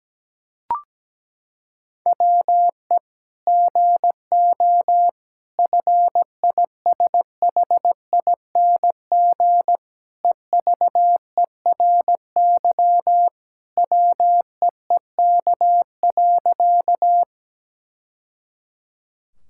Morse code, heard as a keyed tone, about 17 wpm, 700 Hz. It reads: WE GO FISHING EVERY WEEK.
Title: Sentences from Top 1000 Words - Review Part 5 - 17wpm